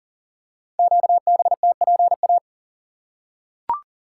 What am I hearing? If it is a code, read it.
Morse code, heard as a keyed tone, about 42 wpm, 690 Hz. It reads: QBTPA
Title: Code Groups - 5 Letters - 40wpm